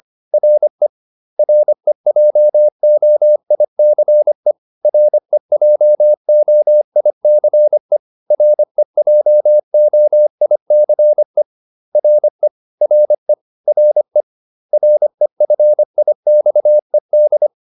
Morse code, heard as a keyed tone, about 25 wpm, 600 Hz. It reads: RE REJOICE REJOICE REJOICE RE RE RE REFIXED